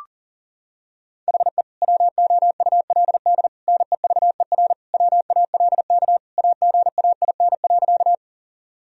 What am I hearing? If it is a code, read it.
Morse code, heard as a keyed tone, about 37 wpm, 700 Hz. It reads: HE WOULD NEVER WALK AGAIN.
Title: Sentences from Top 1000 Words - Review Part 1 - 40wpm